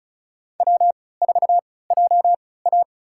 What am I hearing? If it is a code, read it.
Morse code, heard as a keyed tone, about 36 wpm, spaced as for 20 wpm, 700 Hz. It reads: W4JA
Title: USA Call Signs - 4 Characters - 3x Character Spacing - 35wpm